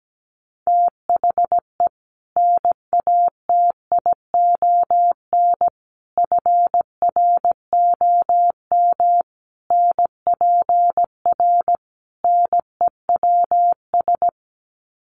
Morse code, heard as a keyed tone, about 17 wpm, 700 Hz. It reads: THE NATION FROM NPR NEWS
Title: Sets of 5 Words 17wpm